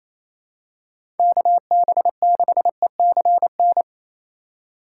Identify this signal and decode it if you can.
Morse code, heard as a keyed tone, about 28 wpm, 700 Hz. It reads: KB6ECN
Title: USA Call Signs - 6 Characters - 4x Character Spacing - 28wpm